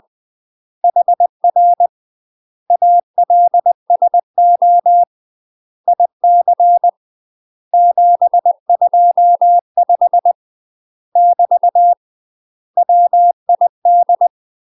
Morse code, heard as a keyed tone, about 20 wpm, 700 Hz. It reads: HR ALSO IC 725 <BT> WID
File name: UK MASTER QSOs TRAINING / 20WPM / UK MASTER QSOs 0008